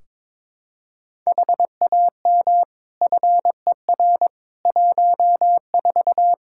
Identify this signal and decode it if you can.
Morse code, heard as a keyed tone, about 22 wpm, 700 Hz. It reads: HAM FER 14
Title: Practice QSO - Exchange 3 - More Info - 22wpm